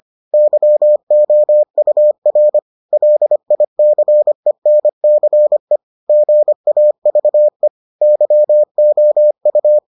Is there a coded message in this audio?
Yes: Morse code from 0.3 to 9.8 s, heard as a keyed tone, about 25 wpm, 600 Hz. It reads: YOUR LICENCE GAVE YOU